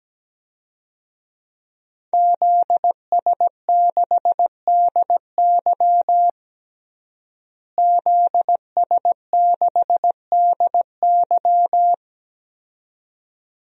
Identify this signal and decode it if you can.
Morse code, heard as a keyed tone, about 17 wpm, 700 Hz. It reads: ZS6DY ZS6DY
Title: DXCC-100 Call Signs - 5 Characters - Repeated 2x - 17wpm